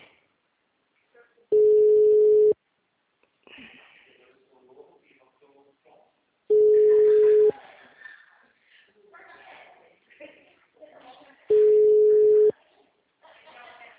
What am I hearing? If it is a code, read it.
Ringback tone (Europe, 425 Hz), with speech.
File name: out- -2012- - - .29